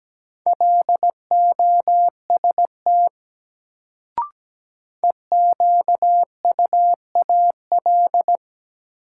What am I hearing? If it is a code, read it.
Morse code, heard as a keyed tone, about 17 wpm, 700 Hz. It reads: LOST EQUAL
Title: Sentences from Top 1400 Words - New Word Drill - 17wpm